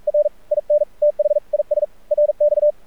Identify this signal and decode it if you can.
Morse code, heard as a keyed tone, about 45 wpm, 600 Hz. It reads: R IN THIS RX
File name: archived / CRSnet